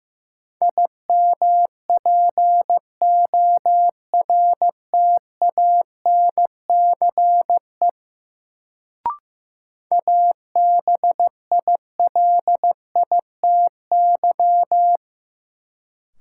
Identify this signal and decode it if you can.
Morse code, heard as a keyed tone, about 15 wpm, 700 Hz. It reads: IMPORTANCE ABILITY
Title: Sets of 3 Words - New Word Drill - 15wpm